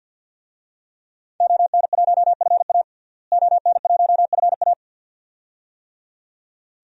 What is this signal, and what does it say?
Morse code, heard as a keyed tone, about 52 wpm, 700 Hz. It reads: ON1FA ON1FA
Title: DXCC-100 Call Signs - 5 Characters - Repeated 2x - 50wpm